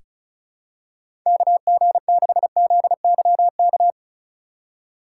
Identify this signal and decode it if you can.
Morse code, heard as a keyed tone, about 36 wpm, 700 Hz. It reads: KG6ZYK
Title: USA Call Signs - 6 Characters - 2x Character Spacing - 35wpm